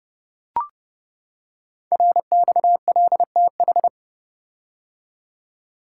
Morse code, heard as a keyed tone, about 29 wpm, 700 Hz. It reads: RXLTH
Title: Code Groups - 5 Letters - 30wpm